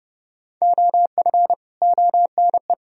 Morse code, heard as a keyed tone, about 34 wpm, 700 Hz. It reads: OF ONE